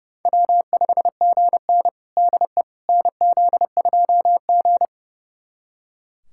Morse code, heard as a keyed tone, about 32 wpm, 700 Hz. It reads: W5GN DE NZ2G
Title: USA Callsign DE Callsign - 4 Characters - 30wpm